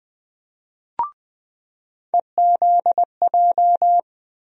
Morse code, heard as a keyed tone, about 20 wpm, 700 Hz. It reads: EZJ
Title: USA Call Sign Postfixes - 3 Letters 20wpm